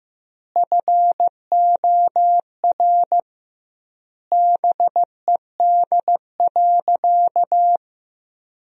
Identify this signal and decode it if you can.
Morse code, heard as a keyed tone, about 15 wpm, 700 Hz. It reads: FOR BED.
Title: 2x Word-Spacing - Sentences from Top 500 Words - 15wpm